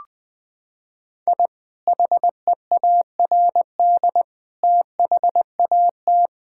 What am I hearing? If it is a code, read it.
Morse code, heard as a keyed tone, about 20 wpm, 700 Hz. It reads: I HEARD THAT
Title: Sentences with 2-Word Phrases - 20wpm